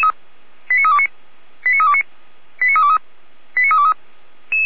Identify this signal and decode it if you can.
Five-tone selective call (ZVEI), heard as quick runs of five tones, tones of about 80 ms. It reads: 89329 89329 89323 89323